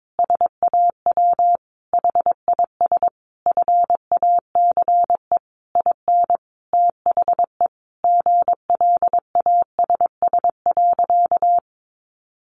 Morse code, heard as a keyed tone, about 22 wpm, 700 Hz. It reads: SAW HIS FACE IN THE GLASS.
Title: Sentences from Top 1300 Words 22wpm